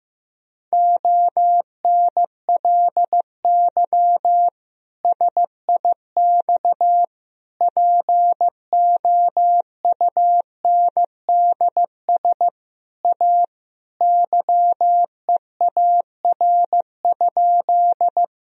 Morse code, heard as a keyed tone, about 16 wpm, 700 Hz. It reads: ONLY SIX POUNDS A YEAR?
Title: Sentences from Top 1000 Words - Review Part 2 - 15wpm